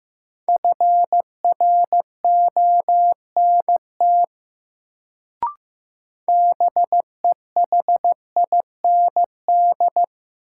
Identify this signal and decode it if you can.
Morse code, heard as a keyed tone, about 15 wpm, 700 Hz. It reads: FRONT BEHIND
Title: Sentences from Top 500 Words - New Word Drill - 15wpm